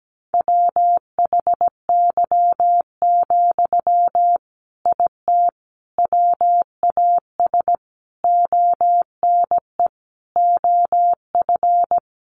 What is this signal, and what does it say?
Morse code, heard as a keyed tone, about 17 wpm, 700 Hz. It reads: WHY, IT WAS ONE OF
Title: Sentences from Top 300 Words 17wpm